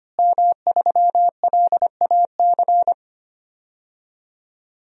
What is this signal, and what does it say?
Morse code, heard as a keyed tone, about 25 wpm, 700 Hz. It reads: M3LAC
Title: DXCC-100 Call Signs - 5 Characters - Repeated 3x - 25wpm